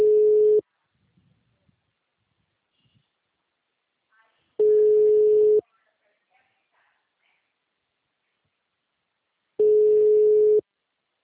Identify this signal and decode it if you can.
Ringback tone (Europe, 425 Hz).
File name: out- -2023- - - .75318